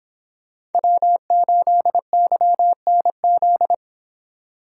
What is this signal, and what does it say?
Morse code, heard as a keyed tone, about 25 wpm, 700 Hz. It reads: W8YNZ